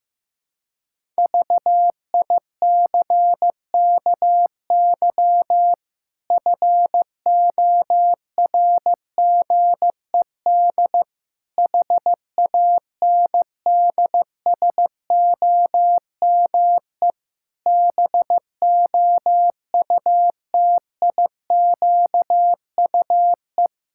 Morse code, heard as a keyed tone, about 15 wpm, 700 Hz. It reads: VICKY FORGED HANDSOME BOUTIQUE
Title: Pangram Challenge v2 - 15wpm